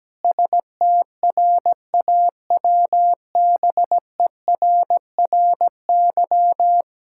Morse code, heard as a keyed tone, about 17 wpm, 700 Hz. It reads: STRAWBERRY
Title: Pangram Challenge v2 - 17wpm